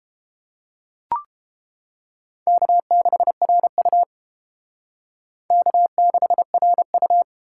Morse code, heard as a keyed tone, about 28 wpm, 700 Hz. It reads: K6RU K6RU